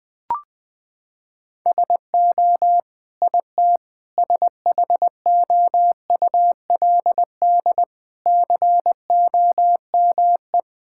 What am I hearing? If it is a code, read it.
Morse code, heard as a keyed tone, about 20 wpm, 700 Hz. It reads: SO IT SHOULD COME
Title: Sets of 8 Words - 20wpm